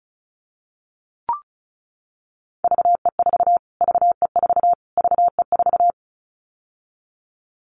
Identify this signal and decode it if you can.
Morse code, heard as a keyed tone, about 34 wpm, 700 Hz. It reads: VE4 VE4 VE4
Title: Callsign Prefixes for Top 25 DXCC - Repeated 3x - 35wpm